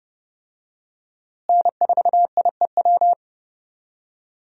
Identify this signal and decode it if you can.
Morse code, heard as a keyed tone, about 30 wpm, 700 Hz. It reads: N4IEW
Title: USA Call Signs - 5 Characters - Repeated 2x - 30wpm